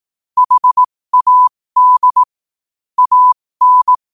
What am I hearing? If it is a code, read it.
Morse code, heard as a keyed tone, about 18 wpm, spaced as for 15 wpm, 1000 Hz. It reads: HAD AN
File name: qotdfiles / 15wpm